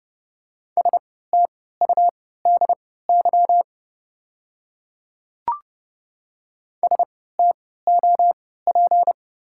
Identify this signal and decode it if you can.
Morse code, heard as a keyed tone, about 30 wpm, spaced as for 17 wpm, 700 Hz. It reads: STUDY STOP
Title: 3x Character Spacing - Top 300 Words - 30wpm